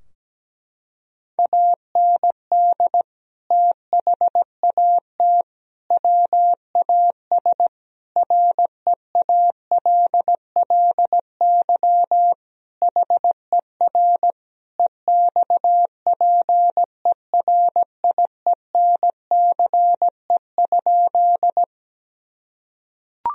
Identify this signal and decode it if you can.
Morse code, heard as a keyed tone, about 17 wpm, 700 Hz. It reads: AND THAT WAS REALLY HER EXPERIENCE?
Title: Sentences from Top 1000 Words - Review Part 3 - 17wpm